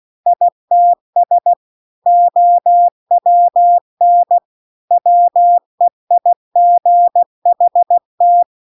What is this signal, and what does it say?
Morse code, heard as a keyed tone, about 16 wpm, 700 Hz. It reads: ITS OWN WEIGHT
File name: CW-The-Big-Twoheaded-River-2-16WPM